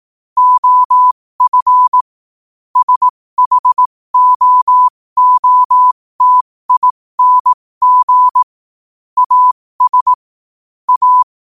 Morse code, heard as a keyed tone, about 18 wpm, spaced as for 15 wpm, 1000 Hz. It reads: OF SHOOTING AS A
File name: qotdfiles / 15wpm